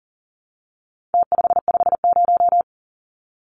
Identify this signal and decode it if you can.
Morse code, heard as a keyed tone, about 40 wpm, 700 Hz. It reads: T550
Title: Numbers - Contest Style - 4-Digit 40wpm